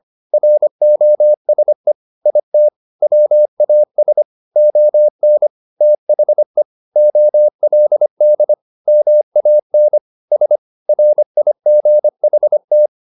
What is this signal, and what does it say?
Morse code, heard as a keyed tone, about 25 wpm, 600 Hz. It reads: ROSE IT WAS ON THE OLD MAN S RIGHT
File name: the old man and the sea part 1 25wpm